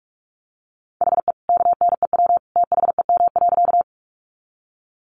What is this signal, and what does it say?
Morse code, heard as a keyed tone, about 44 wpm, 700 Hz. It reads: HE KNEW THEM.
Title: Sentences from Top 400 Words 45wpm